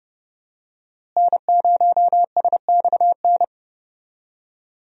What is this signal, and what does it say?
Morse code, heard as a keyed tone, about 30 wpm, 700 Hz. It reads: N0SXN
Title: USA Call Signs - 5 Characters - Repeated 2x - 30wpm